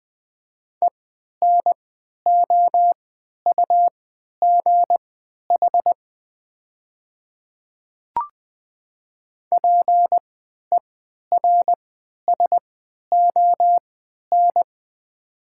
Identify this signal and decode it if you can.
Morse code, heard as a keyed tone, about 20 wpm, spaced as for 11 wpm, 700 Hz. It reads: ENOUGH PERSON
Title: 3x Character Spacing - Top 500 Words - 20wpm